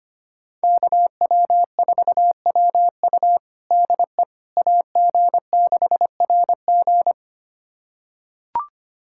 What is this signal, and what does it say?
Morse code, heard as a keyed tone, about 25 wpm, 700 Hz. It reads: KW4WU DE AG6RG
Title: USA Callsign DE Callsign - 5 Characters - 25wpm